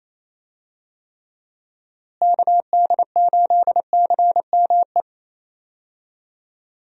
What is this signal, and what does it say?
Morse code, heard as a keyed tone, about 28 wpm, 700 Hz. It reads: KD8CME
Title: USA Call Signs - 6 Characters - Repeated 3x - 28wpm